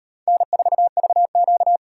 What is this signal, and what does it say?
Morse code, heard as a keyed tone, about 38 wpm, 700 Hz. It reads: N4VQ